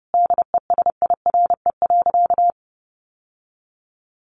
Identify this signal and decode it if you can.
Morse code, heard as a keyed tone, about 30 wpm, 700 Hz. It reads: DESIRE.
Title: Sentences from Top 1100 Words 30wpm